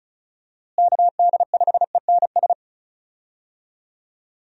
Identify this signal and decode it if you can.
Morse code, heard as a keyed tone, about 36 wpm, 700 Hz. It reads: KD5ENS
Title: USA Call Signs - 6 Characters - Repeated 2x - 35wpm